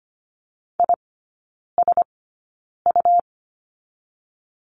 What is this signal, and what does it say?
Morse code, heard as a keyed tone, about 25 wpm, spaced as for 9 wpm, 700 Hz. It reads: ISU
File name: Lesson 17e - Letter u - Three Characters 25wpm